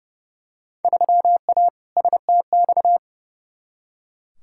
Morse code, heard as a keyed tone, about 32 wpm, 700 Hz. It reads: 3A STX
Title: Field Day Exchanges - 30wpm